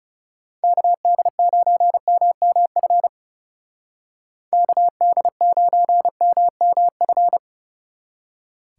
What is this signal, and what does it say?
Morse code, heard as a keyed tone, about 38 wpm, 700 Hz. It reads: KD9MMF KD9MMF
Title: Speed Racing - USA 2x3 Call Signs 30wpm